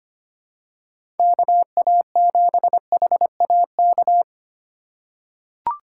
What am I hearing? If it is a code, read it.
Morse code, heard as a keyed tone, about 25 wpm, 700 Hz. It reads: KA7HAK
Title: USA Call Signs - 6 Characters - 2x Character Spacing - 25wpm